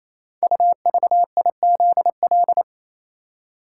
Morse code, heard as a keyed tone, about 27 wpm, 690 Hz. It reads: UVIZL
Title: Code Groups - 5 Letters - 28wpm